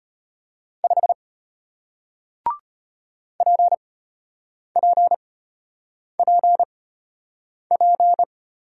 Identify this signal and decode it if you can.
Morse code, heard as a keyed tone, about 33 wpm, spaced as for 8 wpm, 700 Hz. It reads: 5 PPPP